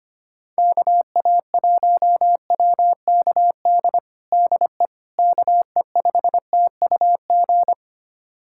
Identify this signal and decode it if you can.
Morse code, heard as a keyed tone, about 25 wpm, 700 Hz. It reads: KA1WKD DE KE5TUG
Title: USA Callsign DE Callsign - 6 Characters - 25wpm